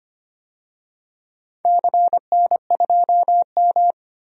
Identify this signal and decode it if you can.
Morse code, heard as a keyed tone, about 25 wpm, 700 Hz. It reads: CN2M
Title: DXCC-100 Call Signs - 4 Characters - 25wpm